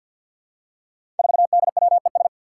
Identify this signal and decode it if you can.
Morse code, heard as a keyed tone, about 50 wpm, 700 Hz. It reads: 4DWES
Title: Five Challenging Letter-Number Combinations 50wpm